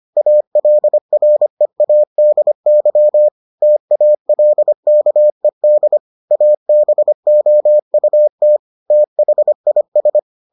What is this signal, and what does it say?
Morse code, heard as a keyed tone, about 25 wpm, 600 Hz. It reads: ALREADY TALKED ABOUT THIS